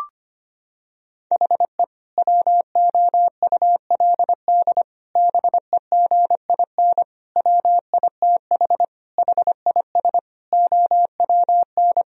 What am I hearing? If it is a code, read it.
Morse code, heard as a keyed tone, about 25 wpm, 700 Hz. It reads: HE WOULD BEGIN WITH HIS OWN
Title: Sentences from Top 1000 Words - Review Part 1 - 25wpm